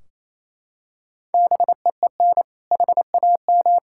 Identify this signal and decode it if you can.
Morse code, heard as a keyed tone, about 27 wpm, 700 Hz. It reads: BEEN HAM
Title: Practice QSO - Exchange 3 - More Info - 28wpm